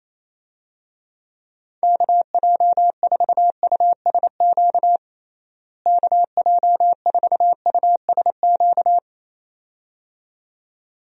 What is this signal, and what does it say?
Morse code, heard as a keyed tone, about 27 wpm, 700 Hz. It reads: KJ4USQ KJ4USQ
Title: USA Call Signs - 6 Characters - Repeated 2x - 28wpm